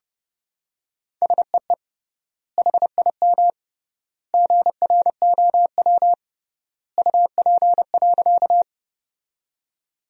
Morse code, heard as a keyed tone, about 31 wpm, 700 Hz. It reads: SEE HIM GROW UP.
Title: 3x Word-Spacing - Sentences from Top 300 Words - 30wpm